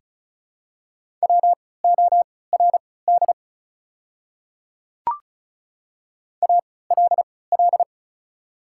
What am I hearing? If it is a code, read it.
Morse code, heard as a keyed tone, about 36 wpm, spaced as for 20 wpm, 700 Hz. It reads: WORD ALL
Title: 3x Character Spacing - Top 100 Words - 35wpm